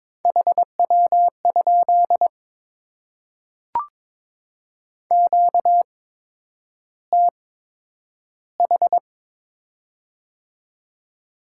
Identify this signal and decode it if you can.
Morse code, heard as a keyed tone, about 22 wpm, spaced as for 9 wpm, 700 Hz. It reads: HW? QTH
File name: Top 20 QSO Elements - 8x Character Spacing - 22wpm